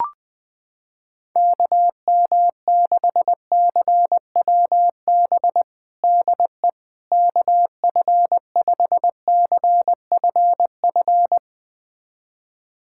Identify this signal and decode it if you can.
Morse code, heard as a keyed tone, about 20 wpm, 700 Hz. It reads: KM6CWB DE KF5CFF
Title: USA Callsign DE Callsign 20wpm